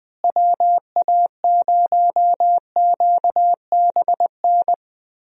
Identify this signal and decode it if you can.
Morse code, heard as a keyed tone, about 21 wpm, 700 Hz. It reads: WA0QBN
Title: USA Call Signs - 6 Characters - Repeated 3x - 20wpm